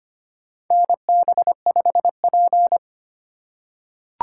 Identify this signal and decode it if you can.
Morse code, heard as a keyed tone, about 25 wpm, 700 Hz. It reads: NB5P
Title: USA Call Signs - 4 Characters - 2x Character Spacing - 25wpm